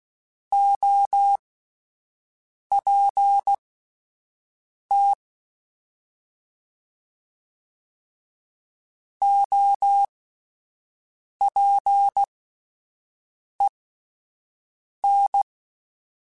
Morse code, heard as a keyed tone, about 16 wpm, spaced as for 5 wpm, 800 Hz. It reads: OPT OPEN
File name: Alphabetical / Alphabetical mp3s / Os @5